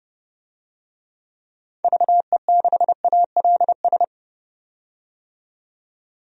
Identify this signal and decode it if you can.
Morse code, heard as a keyed tone, about 30 wpm, 700 Hz. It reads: VE6ALS
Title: DXCC-100 Call Signs - 6 Characters - 30wpm